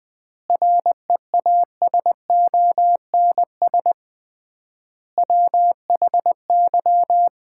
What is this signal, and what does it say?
Morse code, heard as a keyed tone, about 20 wpm, 700 Hz. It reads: REASONS WHY
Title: 3x Word-Spacing - Sets of 5 Words - 20wpm